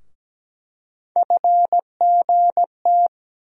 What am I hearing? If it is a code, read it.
Morse code, heard as a keyed tone, about 17 wpm, 700 Hz. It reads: FGT
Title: USA Call Sign Postfixes - 3 Letters 17wpm